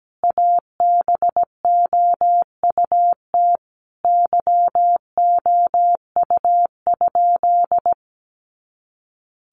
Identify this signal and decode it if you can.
Morse code, heard as a keyed tone, about 17 wpm, 700 Hz. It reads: ABOUT YOU?
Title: Sentences from Top 100 Words 17wpm